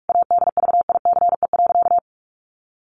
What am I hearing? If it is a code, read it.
Morse code, heard as a keyed tone, about 44 wpm, 700 Hz. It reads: ADVICE.
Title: Sentences from Top 1100 Words 45wpm